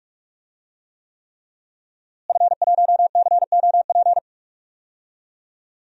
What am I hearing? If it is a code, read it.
Morse code, heard as a keyed tone, about 44 wpm, 700 Hz. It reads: F1COP
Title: DXCC-100 Call Signs - 5 Characters - Repeated 3x - 45wpm